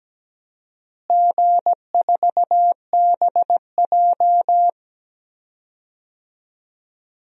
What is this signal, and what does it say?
Morse code, heard as a keyed tone, about 17 wpm, 700 Hz. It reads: G4BJ
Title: DXCC-100 Call Signs - 4 Characters - Repeated 2x - 17wpm